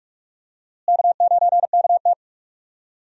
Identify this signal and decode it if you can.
Morse code, heard as a keyed tone, about 46 wpm, 700 Hz. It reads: K9KT
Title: USA Call Signs - 4 Characters - Repeated 2x - 45wpm